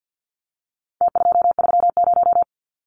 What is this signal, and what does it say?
Morse code, heard as a keyed tone, about 50 wpm, 700 Hz. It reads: T230
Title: Numbers - Contest Style - 4-Digit 50wpm